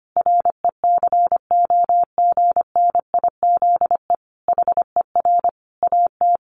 Morse code, heard as a keyed tone, about 25 wpm, 700 Hz. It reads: RECOGNIZE HER AT
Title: Sentences from Top 1400 Words 25wpm